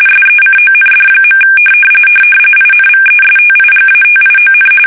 RTTY, about 45.45 baud. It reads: SECOND. INVESTIGATION
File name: fsk850